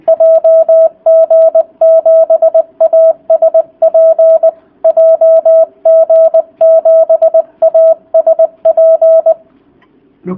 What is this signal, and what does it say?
Morse code, heard as a keyed tone, about 20 wpm, 650 Hz. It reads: JG7ASP JG7ASP